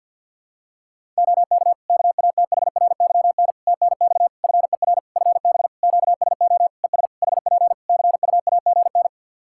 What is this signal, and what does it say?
Morse code, heard as a keyed tone, about 51 wpm, 700 Hz. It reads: OK KATHRYN TNX FER FB QSO ES HP CUAGN
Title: Practice QSO - Exchange 4 - Ending - 50wpm